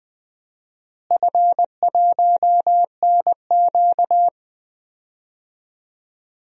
Morse code, heard as a keyed tone, about 21 wpm, 700 Hz. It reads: F1NQ
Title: DXCC-100 Call Signs - 4 Characters - Repeated 2x - 20wpm